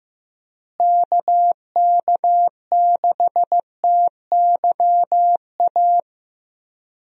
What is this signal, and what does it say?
Morse code, heard as a keyed tone, about 15 wpm, 700 Hz. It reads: KK6TYA
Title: USA Call Signs - 6 Characters - Repeated 2x - 15wpm